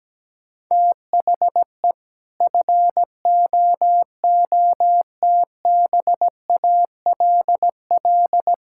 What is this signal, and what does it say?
Morse code, heard as a keyed tone, about 17 wpm, 700 Hz. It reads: THE FOOTBALL
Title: Sentences from Top 1600 Words 17wpm